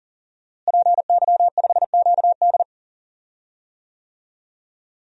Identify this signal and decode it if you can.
Morse code, heard as a keyed tone, about 39 wpm, 700 Hz. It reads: PY5QD
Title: DXCC-100 Call Signs - 5 Characters - Repeated 3x - 40wpm